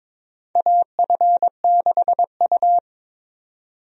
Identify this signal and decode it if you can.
Morse code, heard as a keyed tone, about 22 wpm, 700 Hz. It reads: AF6U
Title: USA Call Signs - 4 Characters - 22wpm